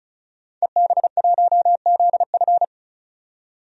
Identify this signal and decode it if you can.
Morse code, heard as a keyed tone, about 34 wpm, 700 Hz. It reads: EB1ZF
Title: DXCC-100 Call Signs - 5 Characters - 35wpm